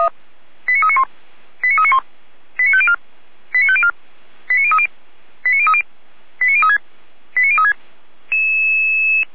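Five-tone selective call (ZVEI), heard as quick runs of five tones, tones of about 70 ms. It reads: 89391 89391 89504 89504 89030 89030 89036 89036
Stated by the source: ZVEI five-tone call received on SDR.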